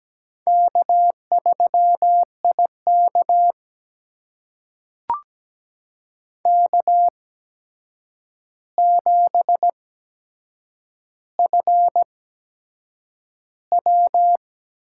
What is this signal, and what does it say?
Morse code, heard as a keyed tone, about 17 wpm, spaced as for 7 wpm, 700 Hz. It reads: K3IK K7FW
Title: USA Call Signs - 4 Characters - 8x Character Spacing - 17wpm